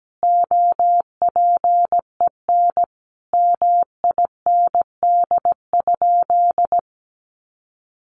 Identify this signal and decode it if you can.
Morse code, heard as a keyed tone, about 17 wpm, 700 Hz. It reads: OPEN MIND?
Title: Sentences from Top 1300 Words 17wpm